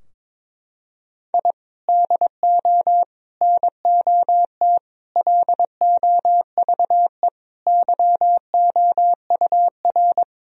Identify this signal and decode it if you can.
Morse code, heard as a keyed tone, about 22 wpm, 700 Hz. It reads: I DO NOT LOVE YOUR
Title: Sentences from Top 1000 Words - Review Part 4 - 22wpm